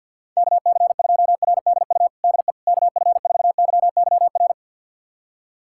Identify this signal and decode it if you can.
Morse code, heard as a keyed tone, about 52 wpm, 700 Hz. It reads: KC2RDU DE KF4YYR
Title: USA Callsign DE Callsign 50wpm